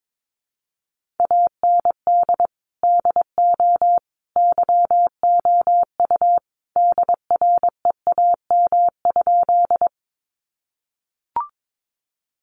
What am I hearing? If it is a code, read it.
Morse code, heard as a keyed tone, about 22 wpm, 700 Hz. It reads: AND DO YOU DREAM?
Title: Sentences from Top 1500 Words 22wpm